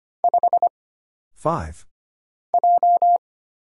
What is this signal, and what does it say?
Morse code, heard as a keyed tone, about 25 wpm, 700 Hz. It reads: 5 J
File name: Single Character (Letter:Number:Punctuation:Prosign) - Mind-Melt - 25wpm